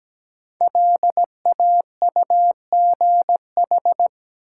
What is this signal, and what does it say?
Morse code, heard as a keyed tone, about 17 wpm, 700 Hz. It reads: LAUGH
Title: Sentences from Top 500 Words - New Word Drill - 17wpm